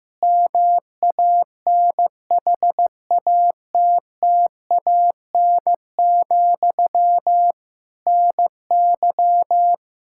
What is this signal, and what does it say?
Morse code, heard as a keyed tone, about 15 wpm, 700 Hz. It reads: MANHATTAN, NY
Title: Practice QSO - Exchange 1 - Beginning - 15wpm